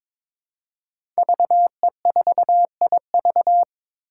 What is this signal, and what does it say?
Morse code, heard as a keyed tone, about 22 wpm, 700 Hz. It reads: VE4IV
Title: Five Challenging Letter-Number Combinations 22wpm